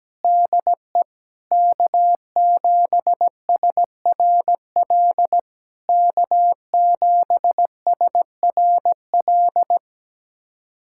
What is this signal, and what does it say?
Morse code, heard as a keyed tone, about 17 wpm, 690 Hz. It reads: DE K7SRL K7SRL
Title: SOTA QSO Practice 17wpm